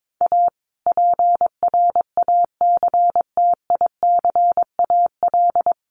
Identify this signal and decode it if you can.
Morse code, heard as a keyed tone, about 22 wpm, 700 Hz. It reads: A PRACTICAL
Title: Sentences from Top 1400 Words 22wpm